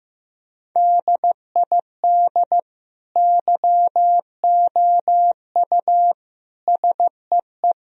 Morse code, heard as a keyed tone, about 15 wpm, 700 Hz. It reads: DID YOU SEE
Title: Sentences from Top 1000 Words - Review Part 4 - 15wpm